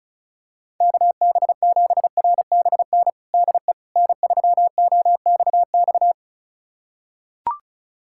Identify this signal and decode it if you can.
Morse code, heard as a keyed tone, about 36 wpm, 700 Hz. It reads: KB7RBN DE N3OXX
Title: USA Callsign DE Callsign 35wpm